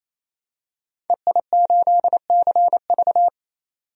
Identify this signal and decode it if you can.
Morse code, heard as a keyed tone, about 28 wpm, 700 Hz. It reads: EI8CV
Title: DXCC-100 Call Signs - 5 Characters - 28wpm